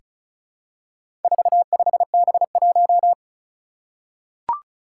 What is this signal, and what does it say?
Morse code, heard as a keyed tone, about 36 wpm, 700 Hz. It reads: 45B1
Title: Four Challenging Letter-Number Combinations 35wpm